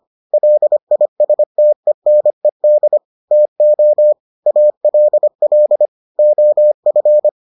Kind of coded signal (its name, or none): Morse code